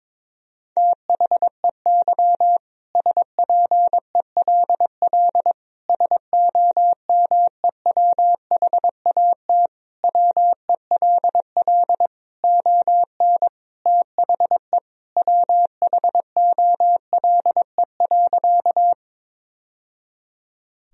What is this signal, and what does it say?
Morse code, heard as a keyed tone, about 22 wpm, 700 Hz. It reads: THEY SPELL SOMEWHAT WELL ON THE WHOLE.
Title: Sentences from Top 1000 Words - Review Part 1 - 22wpm